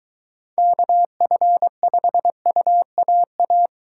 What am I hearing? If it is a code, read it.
Morse code, heard as a keyed tone, about 23 wpm, 700 Hz. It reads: KF5UAA